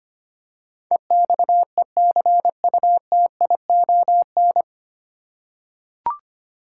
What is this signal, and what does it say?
Morse code, heard as a keyed tone, about 25 wpm, 700 Hz. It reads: EXECUTION
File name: Nine Letter Words - 25wpm